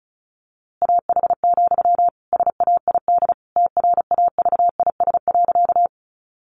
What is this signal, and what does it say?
Morse code, heard as a keyed tone, about 34 wpm, 700 Hz. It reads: AH, SAID TRAVIS.
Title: A Sound Of Thunder 35wpm